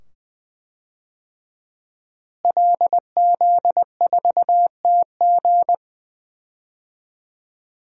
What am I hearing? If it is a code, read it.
Morse code, heard as a keyed tone, about 20 wpm, 700 Hz. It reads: LZ4TG
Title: DXCC-100 Call Signs - 5 Characters - Repeated 2x - 20wpm